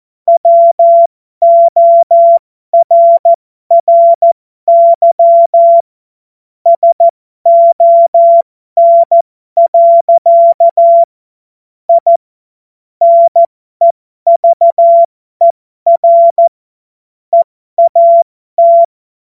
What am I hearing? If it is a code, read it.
Morse code, heard as a keyed tone, about 14 wpm, spaced as for 12 wpm, 680 Hz. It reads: WORRY SON. I NEVER EAT